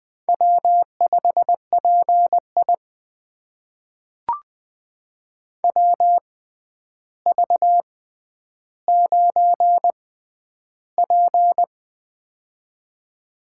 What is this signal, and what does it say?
Morse code, heard as a keyed tone, about 20 wpm, spaced as for 10 wpm, 700 Hz. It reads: W5PI WV9P
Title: USA Call Signs - 4 Characters - 6x Character Spacing - 20wpm